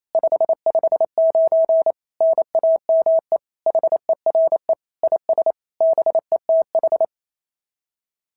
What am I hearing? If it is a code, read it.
Morse code, heard as a keyed tone, about 29 wpm, 690 Hz. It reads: 559 NAME HERE IS BETH
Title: SOTA QSO Practice 28wpm